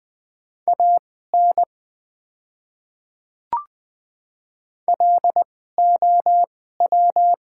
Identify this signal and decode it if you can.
Morse code, heard as a keyed tone, about 20 wpm, spaced as for 14 wpm, 700 Hz. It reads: AN LOW
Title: 2x Character Spacing - Top 200 Words - 20wpm